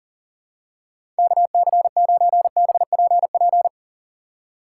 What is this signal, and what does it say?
Morse code, heard as a keyed tone, about 40 wpm, 700 Hz. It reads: KC9BPP
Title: USA Call Signs - 6 Characters - 4x Character Spacing - 40wpm